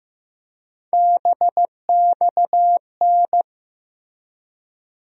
Morse code, heard as a keyed tone, about 15 wpm, 700 Hz. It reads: BXN
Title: USA Call Sign Postfixes - 3 Letters 15wpm